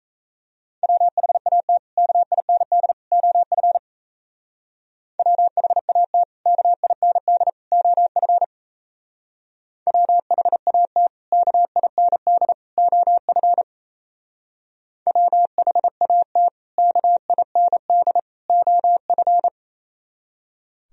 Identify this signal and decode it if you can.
Morse code, heard as a keyed tone, about 33 wpm, 700 Hz. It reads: WHAT KIND OF WHAT KIND OF WHAT KIND OF WHAT KIND OF
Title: Speed-Racing - Sets of 3 Words 28wpm